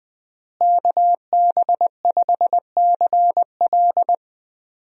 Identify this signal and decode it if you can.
Morse code, heard as a keyed tone, about 20 wpm, 700 Hz. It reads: KB5CL
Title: USA Call Signs - 5 Characters - Repeated 3x - 20wpm